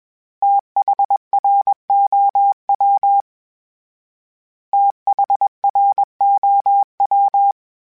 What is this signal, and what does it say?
Morse code, heard as a keyed tone, about 21 wpm, 800 Hz. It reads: THROW THROW
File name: THs / TH / TH Prefixes 3 @20